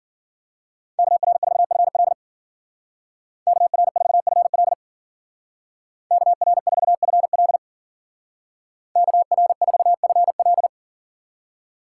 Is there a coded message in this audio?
Yes: Morse code from 1.0 to 10.7 s, heard as a keyed tone, about 53 wpm, 700 Hz. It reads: KR4FL KR4FL KR4FL KR4FL